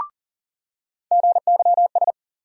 Morse code, heard as a keyed tone, about 40 wpm, 700 Hz. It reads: GYS
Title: USA Call Sign Postfixes - 3 Letters 40wpm